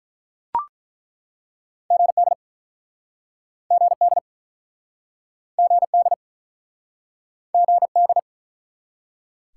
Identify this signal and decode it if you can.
Morse code, heard as a keyed tone, about 43 wpm, 700 Hz. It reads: GD GD GD GD
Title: Speed-Racing Top 90 QSO Elements 35wpm